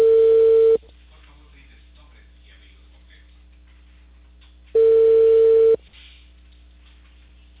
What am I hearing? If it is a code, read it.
Ringback tone (China, 450 Hz), with speech.